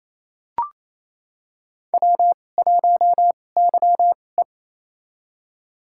Morse code, heard as a keyed tone, about 28 wpm, spaced as for 20 wpm, 700 Hz. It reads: W1YE
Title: USA Call Signs - 4 Characters - 2x Character Spacing - 28wpm